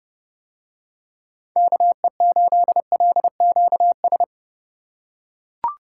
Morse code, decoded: KE8LQS

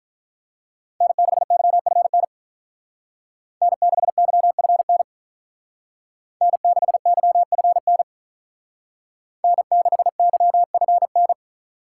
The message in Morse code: N6YFN N6YFN N6YFN N6YFN

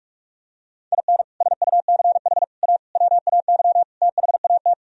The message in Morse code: IN SUCH A WAY THAT

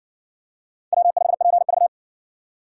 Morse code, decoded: W5PV

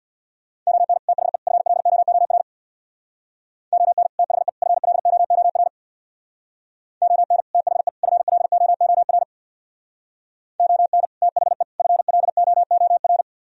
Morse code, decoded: ON THE FLOOR ON THE FLOOR ON THE FLOOR ON THE FLOOR

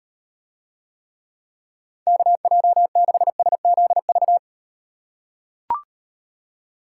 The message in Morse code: KJ6SZV

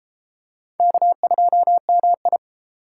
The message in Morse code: K2MI

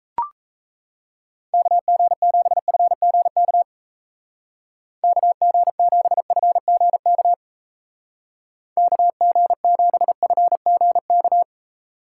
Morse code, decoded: KG7FGK KG7FGK KG7FGK

